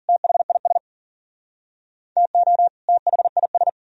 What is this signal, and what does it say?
Morse code, heard as a keyed tone, about 54 wpm, 700 Hz. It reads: THIS TO THIS